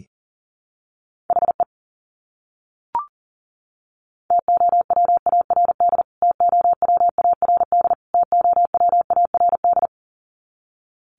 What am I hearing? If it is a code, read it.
Morse code, heard as a keyed tone, about 40 wpm, 700 Hz. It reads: HE TOWARD TOWARD TOWARD
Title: Top 500 Words Repeated 40wpm